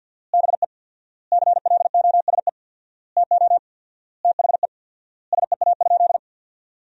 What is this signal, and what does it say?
Morse code, heard as a keyed tone, about 50 wpm, 700 Hz. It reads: BE CLOSE TO THE SEA?
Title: 4x Word-Spacing - Sentences from Top 300 Words - 50wpm